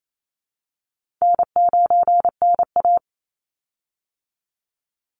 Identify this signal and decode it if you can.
Morse code, heard as a keyed tone, about 28 wpm, 700 Hz. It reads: N9NA
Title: Easy Call Signs Repeated 2x - 28wpm